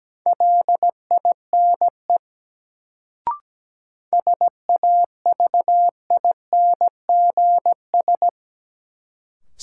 Morse code, heard as a keyed tone, about 17 wpm, 700 Hz. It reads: LINE SAVINGS
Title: Sentences from Top 1200 Words - New Word Drill Alt - 17wpm